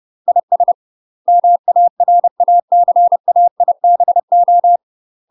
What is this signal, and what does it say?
Morse code, heard as a keyed tone, about 30 wpm, 700 Hz. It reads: IS MARACAIBO
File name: UK MASTER QSOs TRAINING / 30WPM / UK MASTER QSOs 0051